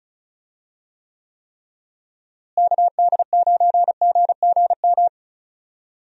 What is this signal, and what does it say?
Morse code, heard as a keyed tone, about 36 wpm, 700 Hz. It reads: KD9GGM
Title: USA Call Signs - 6 Characters - Repeated 3x - 35wpm